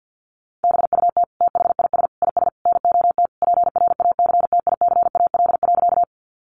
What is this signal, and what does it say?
Morse code, heard as a keyed tone, about 50 wpm, 700 Hz. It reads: BUT THIS IS NOT PRACTICAL.
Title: Sentences from Top 1400 Words 50wpm